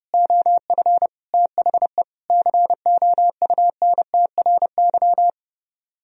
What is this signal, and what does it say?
Morse code, heard as a keyed tone, about 33 wpm, 700 Hz. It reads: OF THE COUNTRY